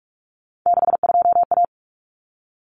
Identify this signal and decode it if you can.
Morse code, heard as a keyed tone, about 44 wpm, 700 Hz. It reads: 62A
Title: Common Cut Numbers - 3-Digit 45wpm